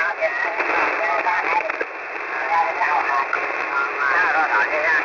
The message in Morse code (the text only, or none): none